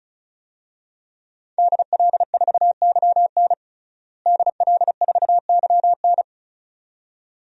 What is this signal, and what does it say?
Morse code, heard as a keyed tone, about 36 wpm, 700 Hz. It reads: DL4YN DL4YN
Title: DXCC-100 Call Signs - 5 Characters - Repeated 2x - 35wpm